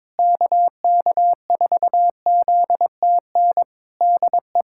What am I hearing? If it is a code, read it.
Morse code, heard as a keyed tone, about 23 wpm, 700 Hz. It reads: KK4ZTN DE
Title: Practice QSO - Exchange 4 - Ending - 22wpm